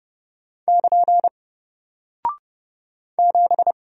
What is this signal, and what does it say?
Morse code, heard as a keyed tone, about 30 wpm, 700 Hz. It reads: <KN> 7
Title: Single Character (Letter:Number:Punctuation:Prosign) Alt 30wpm